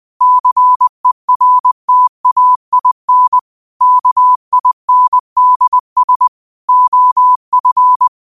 Morse code, heard as a keyed tone, about 20 wpm, 1000 Hz. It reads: CERTAIN KINDS OF